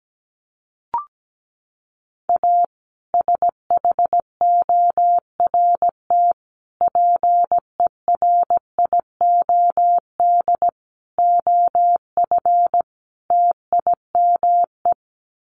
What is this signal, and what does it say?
Morse code, heard as a keyed tone, about 17 wpm, 700 Hz. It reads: A SHORT PERIOD OF TIME
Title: Sets of 5 Words Encore 17wpm